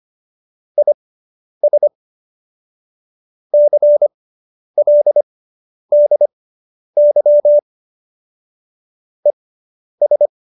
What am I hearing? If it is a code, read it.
Morse code, heard as a keyed tone, about 25 wpm, spaced as for 10 wpm, 590 Hz. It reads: IS CLDY ES
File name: QSO220 10